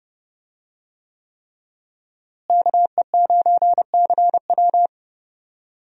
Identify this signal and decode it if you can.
Morse code, heard as a keyed tone, about 30 wpm, 700 Hz. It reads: KE9CW
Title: USA Call Signs - 5 Characters - Repeated 3x - 30wpm